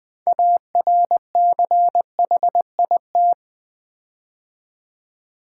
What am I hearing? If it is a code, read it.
Morse code, heard as a keyed tone, about 20 wpm, 700 Hz. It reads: ARCHIT
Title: Build A Word - 6 Letters At A Time - 12-Letter Words - 20wpm